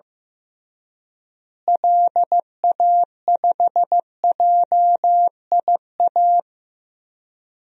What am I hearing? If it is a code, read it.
Morse code, heard as a keyed tone, about 15 wpm, 700 Hz. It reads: LA5JIA
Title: DXCC-100 Call Signs - 6 Characters - Repeated 2x - 15wpm